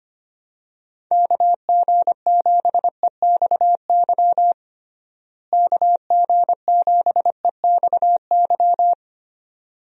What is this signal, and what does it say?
Morse code, heard as a keyed tone, about 25 wpm, 700 Hz. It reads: KG7EXY KG7EXY
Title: USA Call Signs - 6 Characters - Repeated 2x - 25wpm